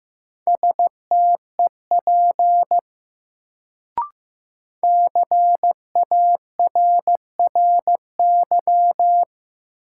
Morse code, heard as a keyed tone, about 15 wpm, 700 Hz. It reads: STEP CARRY
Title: Sentences from Top 400 Words - New Word Drill - 15wpm